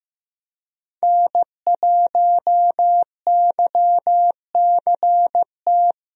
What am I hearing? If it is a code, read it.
Morse code, heard as a keyed tone, about 15 wpm, 700 Hz. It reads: N1YCT
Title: USA Call Signs - 5 Characters - Repeated 3x - 15wpm